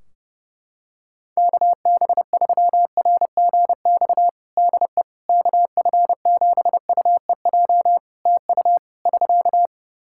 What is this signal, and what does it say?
Morse code, heard as a keyed tone, about 30 wpm, 700 Hz. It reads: KB3RGX DE KF7UEJ TU <SK>
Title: Practice QSO - Exchange 4 - Ending - 30wpm